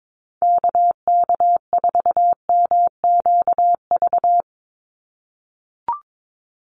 Morse code, heard as a keyed tone, about 22 wpm, 700 Hz. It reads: KK4MQV
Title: USA Call Signs 22wpm